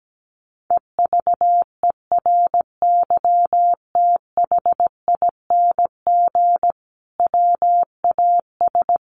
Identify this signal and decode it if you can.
Morse code, heard as a keyed tone, about 17 wpm, 700 Hz. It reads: EVERYTHING WAS
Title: Sentences from Top 900 Words 17wpm